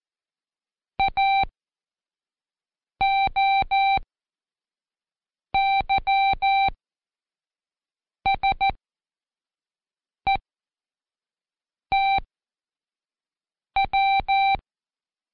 Morse code, heard as a keyed tone, about 14 wpm, 780 Hz. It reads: A O Y S E T W
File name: Seg29-Day4-CheckTest 4